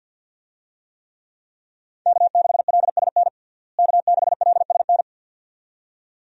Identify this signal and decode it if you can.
Morse code, heard as a keyed tone, about 52 wpm, 700 Hz. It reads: K6LSN K6LSN
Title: USA Call Signs - 5 Characters - Repeated 2x - 50wpm